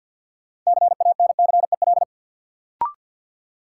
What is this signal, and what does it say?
Morse code, heard as a keyed tone, about 47 wpm, 700 Hz. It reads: CANCEL
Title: Sentences from Top 1200 Words - New Word Drill Alt - 50wpm